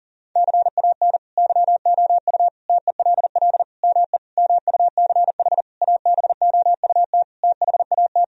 Morse code, heard as a keyed tone, about 37 wpm, 700 Hz. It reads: CAN YOU TELL ME MUCH ABOUT THAT
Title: Sentences from Top 1000 Words - Review Part 4 - 40wpm